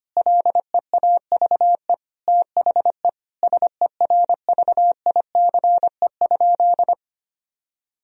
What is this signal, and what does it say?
Morse code, heard as a keyed tone, about 25 wpm, 700 Hz. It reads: LEAVE THE SERVICE?
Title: Sentences from Top 1000 Words - Review Part 4 - 25wpm